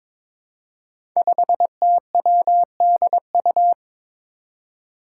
Morse code, heard as a keyed tone, about 22 wpm, 700 Hz. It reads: 5TWDU
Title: Five Challenging Letter-Number Combinations 22wpm